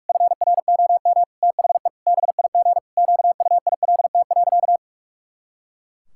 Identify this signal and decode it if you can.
Morse code, heard as a keyed tone, about 41 wpm, 700 Hz. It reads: FROM THE BIG QUILT.
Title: Pangram Challenge v2 - 45wpm